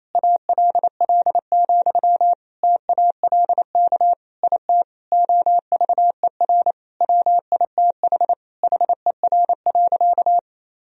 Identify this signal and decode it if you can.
Morse code, heard as a keyed tone, about 28 wpm, 700 Hz. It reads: ALL, TALK IT OVER WITH HER.
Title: Sentences from Top 1000 Words - Review Part 2 - 28wpm